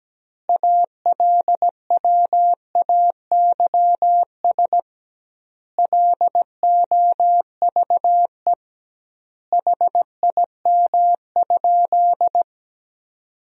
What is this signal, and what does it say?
Morse code, heard as a keyed tone, about 17 wpm, 700 Hz. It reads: ALWAYS LOVE HIM?
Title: 2x Word-Spacing - Sentences from Top 400 Words - 17wpm